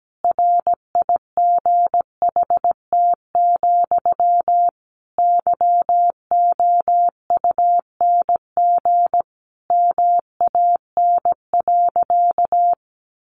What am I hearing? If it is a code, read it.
Morse code, heard as a keyed tone, about 17 wpm, 700 Hz. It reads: RIGHT, YOUNG MAN.
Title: Sentences from Top 400 Words 17wpm